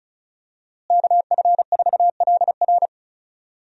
Morse code, heard as a keyed tone, about 36 wpm, 700 Hz. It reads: KF4LR
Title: USA Call Signs - 5 Characters - Repeated 3x - 35wpm